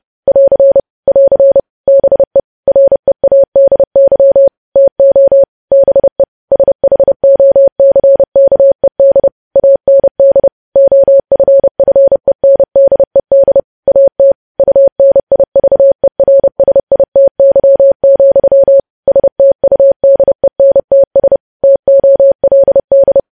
Morse code, heard as a keyed tone, about 30 wpm, 550 Hz. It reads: <AR> <AR> BE READY TO BE SHOCKED AND OFFENDED AT UNIVERSITY, STUDENTS TOLD
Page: news headlines in morse code at 30 wpm